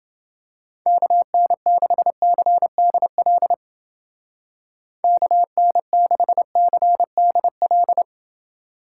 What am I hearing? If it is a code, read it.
Morse code, heard as a keyed tone, about 30 wpm, 700 Hz. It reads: KN6CDL KN6CDL